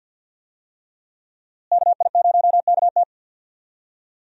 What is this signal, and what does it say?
Morse code, heard as a keyed tone, about 46 wpm, 700 Hz. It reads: KI0KT
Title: USA Call Signs - 5 Characters - Repeated 3x - 50wpm